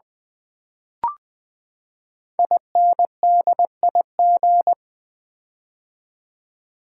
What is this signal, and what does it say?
Morse code, heard as a keyed tone, about 20 wpm, 700 Hz. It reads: INDIG
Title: Build A Word - 5 Letters At A Time - 10-Letter Words - 20wpm